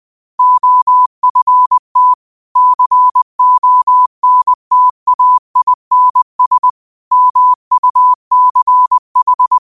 Morse code, heard as a keyed tone, about 20 wpm, 1000 Hz. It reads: OFT CONTAINS MUCH
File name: qotdfiles / 20wpm